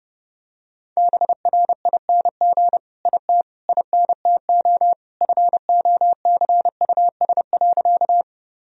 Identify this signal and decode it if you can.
Morse code, heard as a keyed tone, about 32 wpm, 700 Hz. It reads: BRING IT INTO FOCUS.
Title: Sentences from Top 1000 Words - Review Part 4 - 30wpm